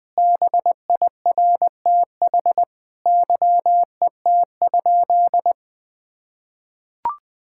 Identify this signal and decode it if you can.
Morse code, heard as a keyed tone, about 20 wpm, 700 Hz. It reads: BIRTH YET?
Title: Sentences from Top 1000 Words - Review Part 5 - 20wpm